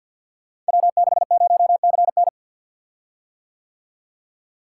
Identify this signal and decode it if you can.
Morse code, heard as a keyed tone, about 48 wpm, 700 Hz. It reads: W60CN
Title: Code Groups - 5 Letter:Numbers - 50wpm